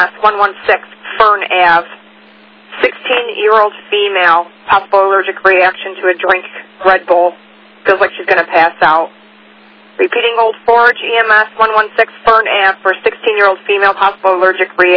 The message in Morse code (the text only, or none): none